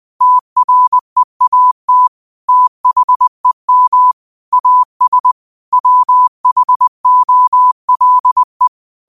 Morse code, heard as a keyed tone, about 20 wpm, 1000 Hz. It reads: TREAT THEM AS WHOLE
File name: qotdfiles / 20wpm